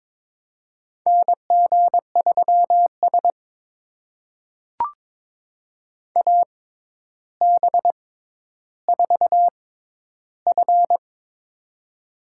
Morse code, heard as a keyed tone, about 22 wpm, spaced as for 11 wpm, 700 Hz. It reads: NG3S AB4F
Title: USA Call Signs - 4 Characters - 6x Character Spacing - 22wpm